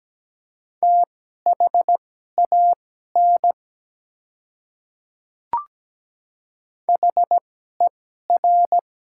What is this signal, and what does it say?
Morse code, heard as a keyed tone, about 17 wpm, spaced as for 12 wpm, 700 Hz. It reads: THAN HER
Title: 2x Character Spacing - Top 200 Words - 17wpm